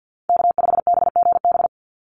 Morse code, heard as a keyed tone, about 50 wpm, 700 Hz. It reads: K5BGB
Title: Call Signs - 50wpm